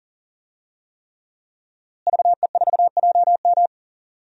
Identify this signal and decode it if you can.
Morse code, heard as a keyed tone, about 40 wpm, 700 Hz. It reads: VE4JM
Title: DXCC-100 Call Signs - 5 Characters - Repeated 3x - 40wpm